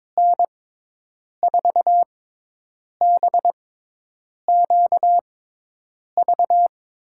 Morse code, heard as a keyed tone, about 21 wpm, spaced as for 8 wpm, 700 Hz. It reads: N4BQV
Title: USA Call Signs - 6x Character Spacing - 22wpm